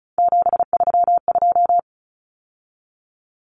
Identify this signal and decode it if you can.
Morse code, heard as a keyed tone, about 34 wpm, 700 Hz. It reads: 732
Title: Numbers - 3-Digit 35wpm